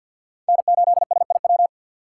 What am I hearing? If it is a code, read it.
Morse code, heard as a keyed tone, about 48 wpm, 700 Hz. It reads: N8SIW
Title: USA Call Signs - 5 Characters - 4x Character Spacing - 50wpm